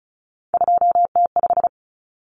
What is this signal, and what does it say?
Morse code, heard as a keyed tone, about 34 wpm, 700 Hz. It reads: 2T5
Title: Common Cut Numbers - 3-Digit 35wpm